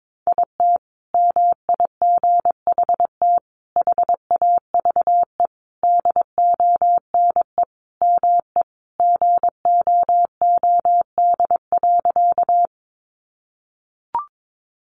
Morse code, heard as a keyed tone, about 22 wpm, 700 Hz. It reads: IT MIGHT HAVE DONE ME GOOD.
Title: Sentences from Top 500 Words 22wpm